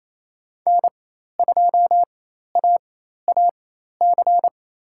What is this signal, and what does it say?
Morse code, heard as a keyed tone, about 28 wpm, spaced as for 13 wpm, 700 Hz. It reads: N2AAC
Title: USA Call Signs - 4x Character Spacing - 28wpm